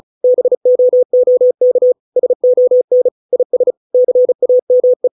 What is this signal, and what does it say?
Morse code, heard as a keyed tone, about 34 wpm, 500 Hz. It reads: BOOK SON IS CAME